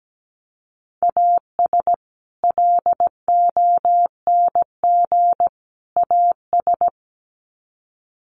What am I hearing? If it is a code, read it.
Morse code, heard as a keyed tone, about 17 wpm, 700 Hz. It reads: AS LONG AS
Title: Sets of 3 Words 17wpm